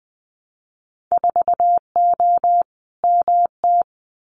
Morse code, heard as a keyed tone, about 20 wpm, 700 Hz. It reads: 4O MT
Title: Winter Field Day Exchanges Encore 20wpm